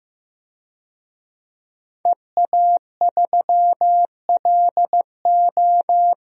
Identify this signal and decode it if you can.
Morse code, heard as a keyed tone, about 15 wpm, 700 Hz. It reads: EA3LO
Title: DXCC-100 Call Signs - 5 Characters - Repeated 3x - 15wpm